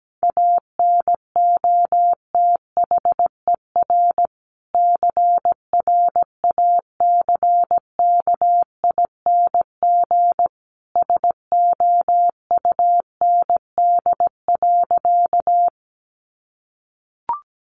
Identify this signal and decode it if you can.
Morse code, heard as a keyed tone, about 17 wpm, 700 Hz. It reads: ANOTHER CRACKING SOUND.
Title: A Sound Of Thunder 17wpm